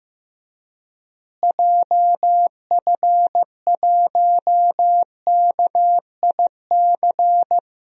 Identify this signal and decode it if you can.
Morse code, heard as a keyed tone, about 15 wpm, 700 Hz. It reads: JF1KIC
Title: DXCC-100 Call Signs - 6 Characters - Repeated 3x - 15wpm